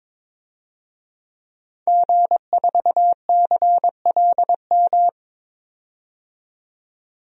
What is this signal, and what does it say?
Morse code, heard as a keyed tone, about 22 wpm, 700 Hz. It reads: G4CLM
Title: DXCC-100 Call Signs - 5 Characters - 22wpm